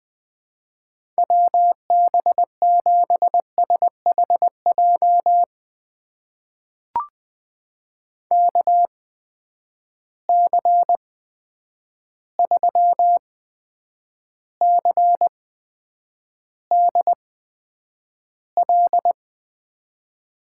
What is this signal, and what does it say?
Morse code, heard as a keyed tone, about 21 wpm, spaced as for 9 wpm, 700 Hz. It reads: WB7SHJ KC3CDL
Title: USA Call Signs - 6 Characters - 8x Character Spacing - 20wpm